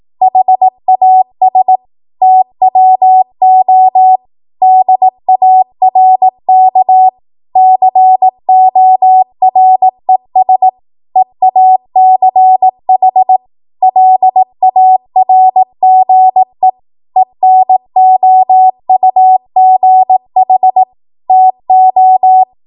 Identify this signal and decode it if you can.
Morse code, heard as a keyed tone, about 18 wpm, 750 Hz. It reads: HAS TWO DARK CORES EACH LARGE ENOUGH TO
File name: Morse / 18WPM